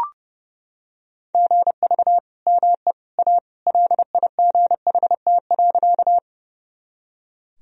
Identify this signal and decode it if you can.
Morse code, heard as a keyed tone, about 32 wpm, 700 Hz. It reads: GV ME A LIGHT.
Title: Sentences from Top 200 Words and Abbreviations 30wpm